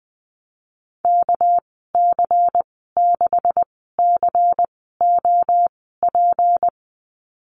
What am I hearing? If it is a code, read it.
Morse code, heard as a keyed tone, about 19 wpm, spaced as for 14 wpm, 700 Hz. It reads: KC6COP
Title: 245 Lesson 39g - Letter z - Call Signs 20wpm